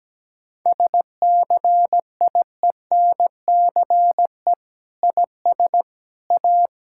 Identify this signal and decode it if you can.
Morse code, heard as a keyed tone, about 17 wpm, 700 Hz. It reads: SCIENCE IS A
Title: Sentences from Top 1000 Words - Review Part 6 - 17wpm